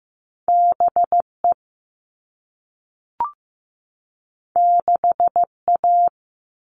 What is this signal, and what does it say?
Morse code, heard as a keyed tone, about 15 wpm, 700 Hz. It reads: BE 6A
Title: All Cut Numbers - 2-Digit 15wpm